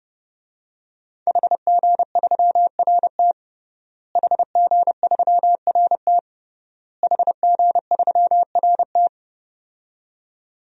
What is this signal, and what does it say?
Morse code, heard as a keyed tone, about 30 wpm, 700 Hz. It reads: HG3RT HG3RT HG3RT
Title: DXCC-100 Call Signs - 5 Characters - Repeated 3x - 30wpm